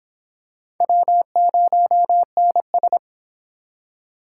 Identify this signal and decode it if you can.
Morse code, heard as a keyed tone, about 25 wpm, 700 Hz. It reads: W0NS